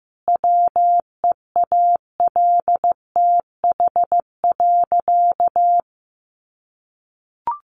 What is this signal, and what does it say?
Morse code, heard as a keyed tone, about 15 wpm, 700 Hz. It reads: WEALTH.
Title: Sentences from Top 1400 Words 15wpm